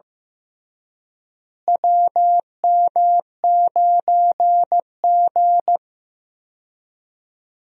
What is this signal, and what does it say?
Morse code, heard as a keyed tone, about 15 wpm, 700 Hz. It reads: WM9G
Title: USA Call Signs - 4 Characters - Repeated 3x - 15wpm